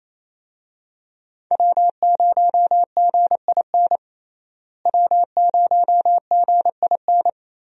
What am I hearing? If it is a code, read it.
Morse code, heard as a keyed tone, about 28 wpm, 700 Hz. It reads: W0GIN W0GIN
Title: USA Call Signs - 5 Characters - Repeated 2x - 28wpm